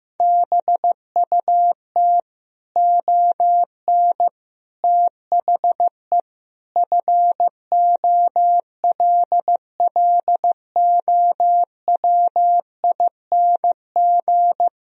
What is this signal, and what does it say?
Morse code, heard as a keyed tone, about 15 wpm, 700 Hz. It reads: BUT ON THE FOLLOWING
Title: Sentences from Top 1000 Words - Review Part 4 - 15wpm